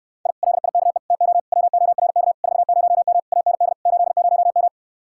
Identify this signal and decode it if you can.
Morse code, heard as a keyed tone, about 68 wpm, 700 Hz. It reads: I LIKE TO WORK 40M AND 80M